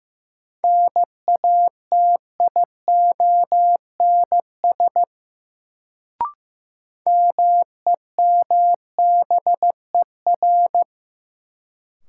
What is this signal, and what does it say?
Morse code, heard as a keyed tone, about 15 wpm, 700 Hz. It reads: NATIONS MEMBER
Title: Sets of 3 Words - New Word Drill - 15wpm